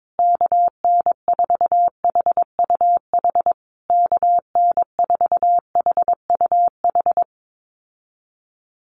Morse code, heard as a keyed tone, about 22 wpm, 700 Hz. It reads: KN4HUH KN4HUH
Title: USA Call Signs Repeated 3x 22wpm